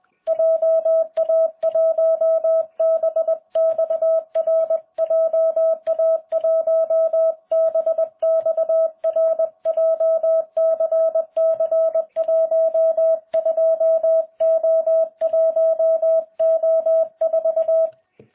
Morse code, heard as a keyed tone, about 21 wpm, 650 Hz. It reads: JA1BXR JA1BXR JCC12O1O4